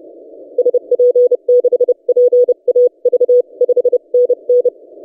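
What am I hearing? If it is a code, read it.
Morse code, heard as a keyed tone, about 30 wpm, 490 Hz. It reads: SP6PAV 5NN
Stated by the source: DX station 3D2GC on 20 metres CW, recorded by N6ML.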